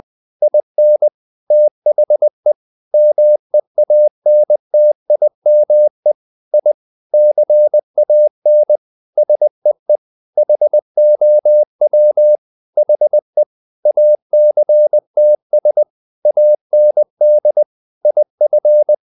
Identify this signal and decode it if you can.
Morse code, heard as a keyed tone, about 20 wpm, 600 Hz. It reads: IN THE MEANTIME I CAN SEE HOW HE ACTS AND IF
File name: the old man and the sea part 2 20wpm